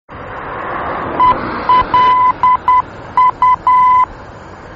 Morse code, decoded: ELU